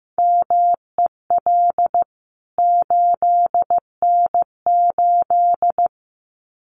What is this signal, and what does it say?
Morse code, heard as a keyed tone, about 15 wpm, 700 Hz. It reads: MEL 8N8
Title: CWT Exchange 15wpm